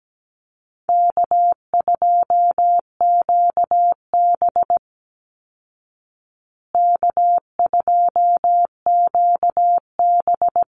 Morse code, heard as a keyed tone, about 17 wpm, 700 Hz. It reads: K2QB K2QB
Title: Call Signs Repeated 2x - 17wpm